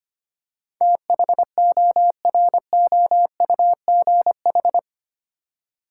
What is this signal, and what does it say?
Morse code, heard as a keyed tone, about 25 wpm, 700 Hz. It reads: THOROUGH
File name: Eight Letter Words - 25wpm